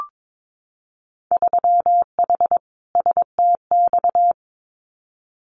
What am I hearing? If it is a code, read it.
Morse code, heard as a keyed tone, about 22 wpm, 700 Hz. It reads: 3H STX
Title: Winter Field Day Exchanges Encore 22wpm